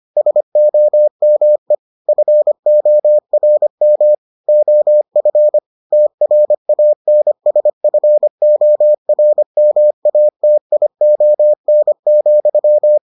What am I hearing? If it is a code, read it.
Morse code, heard as a keyed tone, about 25 wpm, 600 Hz. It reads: SOME FORM OF TRANSFORMATION,